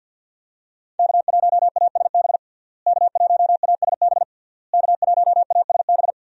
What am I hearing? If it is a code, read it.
Morse code, heard as a keyed tone, about 52 wpm, 700 Hz. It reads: K1ASB K1ASB K1ASB
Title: USA Call Signs - 5 Characters - Repeated 3x - 50wpm